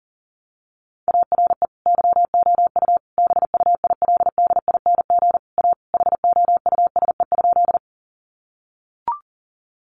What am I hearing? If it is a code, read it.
Morse code, heard as a keyed tone, about 37 wpm, 700 Hz. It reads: ARE YOU BUILDING A HOUSE?
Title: Sentences from Top 700 Words 40wpm